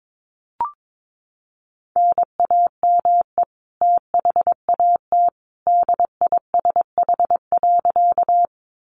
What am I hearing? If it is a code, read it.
Morse code, heard as a keyed tone, about 22 wpm, 700 Hz. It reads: NAME THAT DISH.
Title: Sentences from Top 1300 Words 22wpm